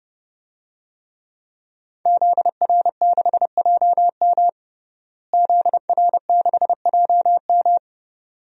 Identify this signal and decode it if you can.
Morse code, heard as a keyed tone, about 32 wpm, 700 Hz. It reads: ZR6JM ZR6JM
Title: DXCC-100 Call Signs - 5 Characters - Repeated 2x - 30wpm